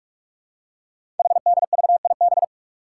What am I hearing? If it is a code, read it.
Morse code, heard as a keyed tone, about 44 wpm, 700 Hz. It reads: HDVIB
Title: Code Groups - 5 Letters - 45wpm